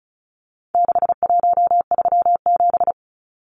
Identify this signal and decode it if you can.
Morse code, heard as a keyed tone, about 34 wpm, 700 Hz. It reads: 6137
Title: Numbers - 4-Digit 35wpm